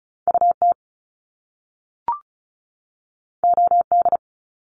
Morse code, heard as a keyed tone, about 36 wpm, 700 Hz. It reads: UT OD
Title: Top Two Letter Combinations in Words 35wpm